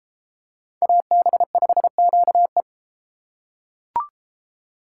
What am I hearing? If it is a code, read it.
Morse code, heard as a keyed tone, about 32 wpm, 700 Hz. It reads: AB5QE